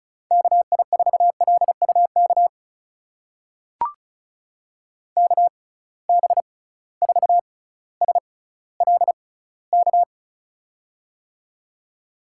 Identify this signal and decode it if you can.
Morse code, heard as a keyed tone, about 34 wpm, spaced as for 18 wpm, 700 Hz. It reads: KI4LUK KB4SLK
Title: USA Call Signs - 6 Characters - 6x Character Spacing - 35wpm